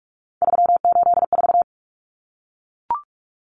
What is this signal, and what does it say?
Morse code, heard as a keyed tone, about 44 wpm, 700 Hz. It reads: 384
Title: Common Cut Numbers - 3-Digit 45wpm